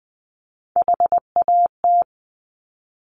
Morse code, heard as a keyed tone, about 20 wpm, 700 Hz. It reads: HAT
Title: Top Three Letter Combinations in Words 20wpm